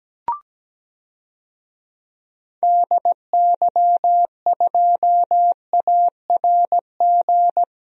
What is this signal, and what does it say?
Morse code, heard as a keyed tone, about 17 wpm, 700 Hz. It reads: DY2ARG
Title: DXCC-100 Call Signs - 6 Characters - Repeated 3x - 17wpm